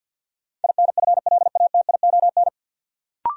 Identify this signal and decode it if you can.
Morse code, heard as a keyed tone, about 48 wpm, 700 Hz. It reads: INFLATION
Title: Sentences from Top 1100 Words - New Word Drill - 50wpm